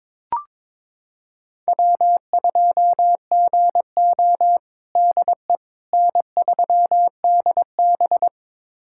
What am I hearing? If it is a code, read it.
Morse code, heard as a keyed tone, about 21 wpm, 700 Hz. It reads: W2GO DE N3DB
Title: USA Callsign DE Callsign - 4 Characters - 22wpm